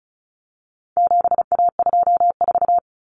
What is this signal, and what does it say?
Morse code, heard as a keyed tone, about 34 wpm, 700 Hz. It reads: 7A24
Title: Common Cut Numbers - 4-Digit 35wpm